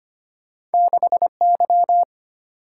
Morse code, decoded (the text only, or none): 6Y